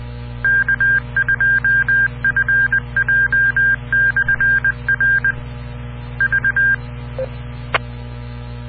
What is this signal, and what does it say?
Morse code, heard as a keyed tone, about 20 wpm, 1600 Hz. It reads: K2FJ/R V